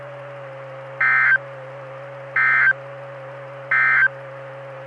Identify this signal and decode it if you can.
Emergency Alert System (SAME) tones, heard as harsh buzzing bursts. It is heard from 0.9 to 4.2 s.
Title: Saline, AK SKYWARN Repeater System - W5RHS